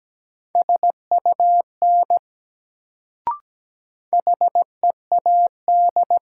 Morse code, heard as a keyed tone, about 17 wpm, 700 Hz. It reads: SUN HEAD
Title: Sentences from Top 300 Words - New Word Drill - 17wpm